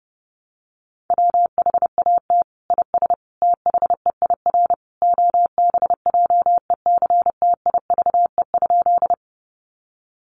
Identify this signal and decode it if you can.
Morse code, heard as a keyed tone, about 30 wpm, 700 Hz. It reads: WHAT IS THEIR OBJECTIVE?
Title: Sentences from Top 1200 Words 30wpm